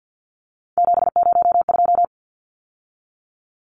Morse code, heard as a keyed tone, about 50 wpm, 700 Hz. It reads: 702
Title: Numbers - 3-Digit 50wpm